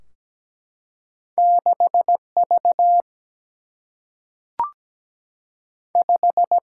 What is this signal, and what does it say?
Morse code, heard as a keyed tone, about 17 wpm, 700 Hz. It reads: 6 V 5
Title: Single-Letter Single-Number 17wpm